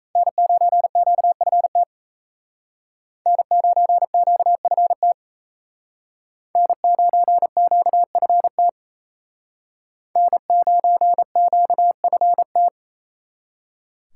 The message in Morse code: N9QFT N9QFT N9QFT N9QFT

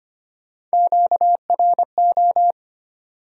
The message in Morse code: QRO

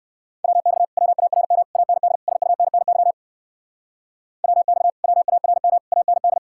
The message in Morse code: WX WARM AND SUNNY WX WARM AND

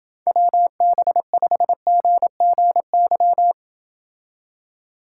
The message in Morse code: WB5GGY

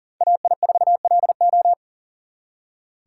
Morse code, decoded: AI4LO

